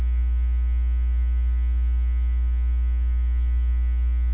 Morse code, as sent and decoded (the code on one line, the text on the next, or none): none